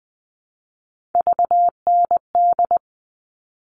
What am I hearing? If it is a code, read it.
Morse code, heard as a keyed tone, about 20 wpm, 700 Hz. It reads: VND
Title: All Cut Numbers - 3-Digit 20wpm